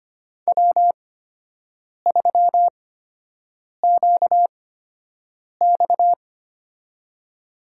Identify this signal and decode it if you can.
Morse code, heard as a keyed tone, about 25 wpm, spaced as for 7 wpm, 700 Hz. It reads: W3QX
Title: USA Call Signs - 4 Characters - 8x Character Spacing - 25wpm